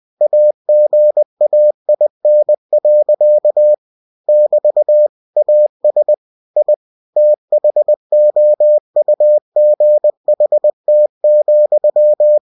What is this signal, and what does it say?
Morse code, heard as a keyed tone, about 20 wpm, 600 Hz. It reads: AGAIN. <BT> AS I THOUGHT,
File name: cw-books / winnie-the-pooh / winnie-20wpm-page-0044